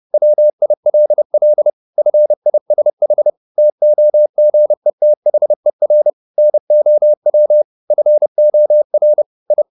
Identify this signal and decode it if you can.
Morse code, heard as a keyed tone, about 30 wpm, 600 Hz. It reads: WILL FISH TOGETHER NOW FOR I